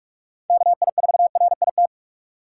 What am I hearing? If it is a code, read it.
Morse code, heard as a keyed tone, about 44 wpm, 700 Hz. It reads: KI4RIT